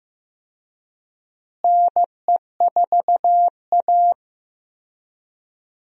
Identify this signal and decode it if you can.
Morse code, heard as a keyed tone, about 16 wpm, 700 Hz. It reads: NE4A
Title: USA Call Signs - 4 Characters - Repeated 3x - 15wpm